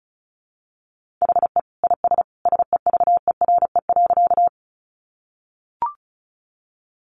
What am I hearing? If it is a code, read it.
Morse code, heard as a keyed tone, about 34 wpm, 700 Hz. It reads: HE IS SEVERE.
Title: Sentences from Top 1400 Words 35wpm